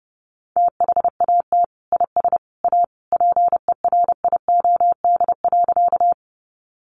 Morse code, heard as a keyed tone, about 30 wpm, 700 Hz. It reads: THAT IS A PERIOD.
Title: Sentences from Top 800 Words 30wpm